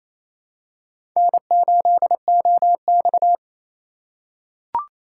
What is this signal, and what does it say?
Morse code, heard as a keyed tone, about 28 wpm, 700 Hz. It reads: N8OX
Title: USA Call Signs - 4 Characters - 4x Character Spacing - 28wpm